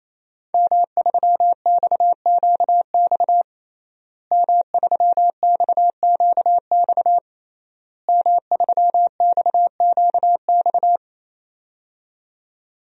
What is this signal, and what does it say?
Morse code, heard as a keyed tone, about 28 wpm, 700 Hz. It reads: M3XQX M3XQX M3XQX
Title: DXCC-100 Call Signs - 5 Characters - Repeated 3x - 28wpm